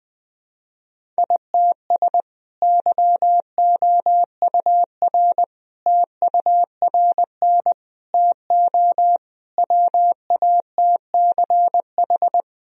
Morse code, heard as a keyed tone, about 20 wpm, 700 Hz. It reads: ITS YOUR TURN TO WATCH
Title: Sentences from Top 1000 Words - Review Part 1 - 20wpm